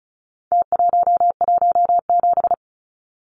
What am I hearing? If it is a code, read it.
Morse code, heard as a keyed tone, about 34 wpm, 700 Hz. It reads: T117
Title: Numbers - Contest Style - 4-Digit 35wpm